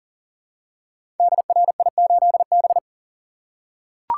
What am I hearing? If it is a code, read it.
Morse code, heard as a keyed tone, about 42 wpm, 700 Hz. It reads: DRI8B
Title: Code Groups - 5 Letter:Numbers - 40wpm